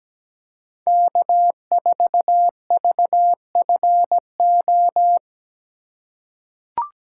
Morse code, heard as a keyed tone, about 17 wpm, 700 Hz. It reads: K4VFO
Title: USA Call Signs - 5 Characters - 3x Character Spacing - 17wpm